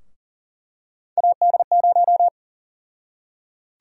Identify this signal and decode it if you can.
Morse code, heard as a keyed tone, about 40 wpm, 700 Hz. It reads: AD0
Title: USA Call Sign Prefixes - Repeated 2x - 40wpm